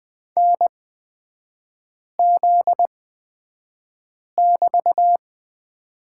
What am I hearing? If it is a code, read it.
Morse code, heard as a keyed tone, about 20 wpm, spaced as for 5 wpm, 700 Hz. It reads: NZ<BT>
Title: 243 Lesson 39e - Letter z - Three Characters 20wpm